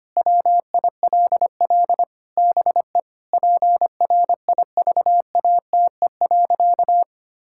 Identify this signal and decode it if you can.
Morse code, heard as a keyed tone, about 25 wpm, 700 Hz. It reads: WILL BE PRIVATE.
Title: Sentences from Top 1600 Words 25wpm